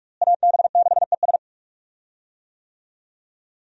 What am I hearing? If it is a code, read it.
Morse code, heard as a keyed tone, about 46 wpm, 700 Hz. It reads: AB6ES
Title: USA Call Signs - 5 Characters - Repeated 2x - 45wpm